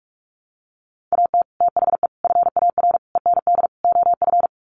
Morse code, heard as a keyed tone, about 44 wpm, 700 Hz. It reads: AT THE FAR END OF
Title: Sets of 5 Words 45wpm